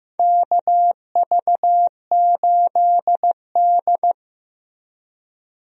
Morse code, heard as a keyed tone, about 15 wpm, 700 Hz. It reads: KV8D
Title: USA Call Signs - 4 Characters - Repeated 2x - 15wpm